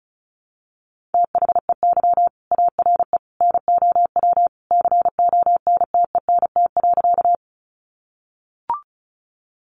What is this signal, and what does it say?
Morse code, heard as a keyed tone, about 34 wpm, 700 Hz. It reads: THEY ARE NOW CONTENT.
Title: Sentences from Top 900 Words 35wpm